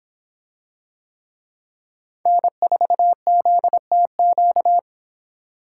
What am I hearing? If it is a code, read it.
Morse code, heard as a keyed tone, about 26 wpm, 700 Hz. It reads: N4ZTQ